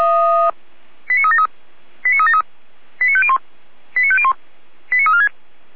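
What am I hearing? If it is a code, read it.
Five-tone selective call (ZVEI), heard as quick runs of five tones, tones of about 70 ms. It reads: 89373 89373 89501 89501 89346
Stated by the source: ZVEI five-tone call received by SDR.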